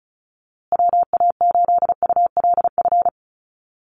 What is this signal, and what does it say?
Morse code, heard as a keyed tone, about 34 wpm, 700 Hz. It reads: WA8ULF
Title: USA Call Signs Repeated 2x 35wpm